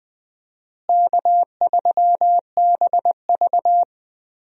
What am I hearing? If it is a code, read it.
Morse code, heard as a keyed tone, about 20 wpm, 700 Hz. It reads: K3BV
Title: USA Call Signs - 4 Characters - Repeated 3x - 20wpm